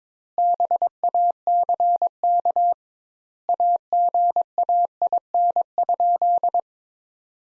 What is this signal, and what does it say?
Morse code, heard as a keyed tone, about 22 wpm, 700 Hz. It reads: BACK AGAIN?
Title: Sentences Top 200 Words - 22wpm